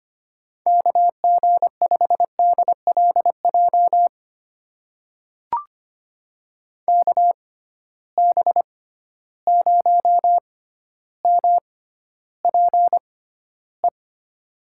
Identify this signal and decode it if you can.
Morse code, heard as a keyed tone, about 25 wpm, spaced as for 13 wpm, 700 Hz. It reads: KG5DLJ KB0MPE
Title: USA Call Signs - 6 Characters - 6x Character Spacing - 25wpm